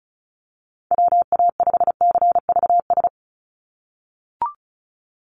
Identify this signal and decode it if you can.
Morse code, heard as a keyed tone, about 34 wpm, 700 Hz. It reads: WA5CVS
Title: USA Call Signs 35wpm